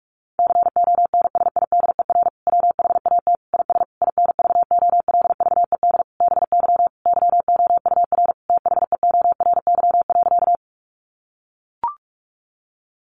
Morse code, heard as a keyed tone, about 44 wpm, 700 Hz. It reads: CONSIDER WHAT IS INVOLVED BY YOUR THEORY.
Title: Sentences from Top 800 Words 45wpm